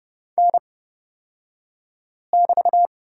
Morse code, heard as a keyed tone, about 30 wpm, spaced as for 5 wpm, 700 Hz. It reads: N<BT>
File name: Lesson 37d - Letter x - Two Characters 30wpm